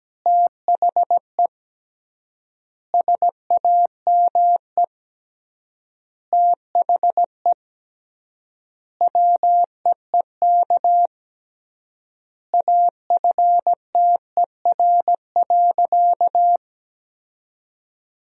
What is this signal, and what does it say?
Morse code, heard as a keyed tone, about 17 wpm, 700 Hz. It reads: THE SAME THE WEEK AFTER.
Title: 3x Word-Spacing - Sentences from Top 500 Words - 17wpm